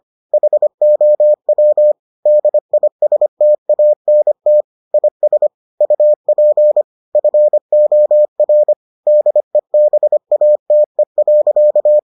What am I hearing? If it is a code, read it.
Morse code, heard as a keyed tone, about 25 wpm, 600 Hz. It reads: HOW DISTANT IS UP FOR DEBATE.